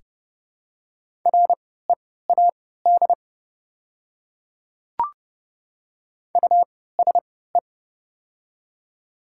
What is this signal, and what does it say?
Morse code, heard as a keyed tone, about 30 wpm, spaced as for 17 wpm, 700 Hz. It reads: READ USE
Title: 3x Character Spacing - Top 200 Words - 30wpm